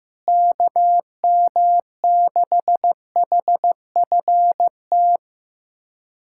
Morse code, heard as a keyed tone, about 15 wpm, 700 Hz. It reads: KM6HFT
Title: USA Callsign DE Callsign - 6 Characters - 15wpm